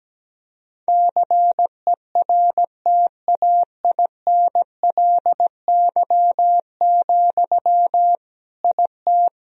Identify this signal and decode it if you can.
Morse code, heard as a keyed tone, about 17 wpm, 700 Hz. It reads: CERTAINLY, IT
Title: Sentences from Top 1000 Words - Review Part 2 - 17wpm